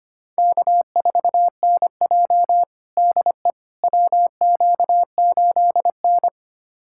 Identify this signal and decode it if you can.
Morse code, heard as a keyed tone, about 25 wpm, 700 Hz. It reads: K4NJ DE WQ8N
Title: USA Callsign DE Callsign - 4 Characters - 25wpm